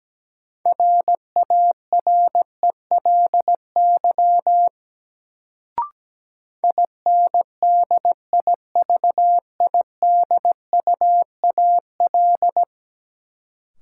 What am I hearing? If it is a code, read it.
Morse code, heard as a keyed tone, about 17 wpm, 700 Hz. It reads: RARELY INDIVIDUAL
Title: Sentences from Top 1400 Words - New Word Drill Alt - 17wpm